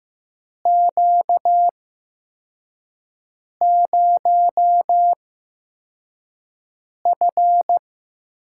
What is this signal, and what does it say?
Morse code, heard as a keyed tone, about 15 wpm, spaced as for 4 wpm, 700 Hz. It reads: Q0F
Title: USA Call Signs - 4 Characters - 8x Character Spacing - 15wpm